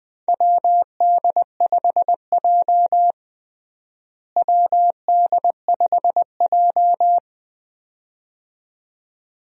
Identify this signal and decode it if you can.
Morse code, heard as a keyed tone, about 20 wpm, 700 Hz. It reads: WD5J WD5J
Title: USA Call Signs - 4 Characters - Repeated 2x - 20wpm